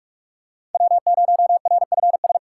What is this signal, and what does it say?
Morse code, heard as a keyed tone, about 46 wpm, 700 Hz. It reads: W0RFS
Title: USA Call Signs - 5 Characters - 6x Character Spacing - 45wpm